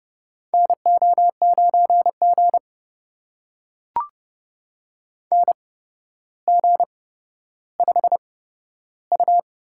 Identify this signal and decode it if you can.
Morse code, heard as a keyed tone, about 31 wpm, spaced as for 13 wpm, 700 Hz. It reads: NO9G NG5U
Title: USA Call Signs - 4 Characters - 8x Character Spacing - 30wpm